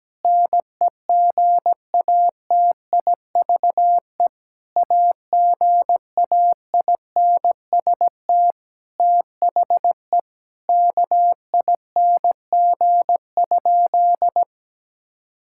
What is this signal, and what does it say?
Morse code, heard as a keyed tone, about 17 wpm, 700 Hz. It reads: NEGATIVE AGAINST THE KING?
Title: Sentences from Top 1000 Words - Review Part 6 - 17wpm